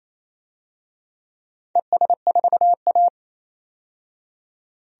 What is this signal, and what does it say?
Morse code, heard as a keyed tone, about 28 wpm, 700 Hz. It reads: ES4A
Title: DXCC-100 Call Signs - 4 Characters - 28wpm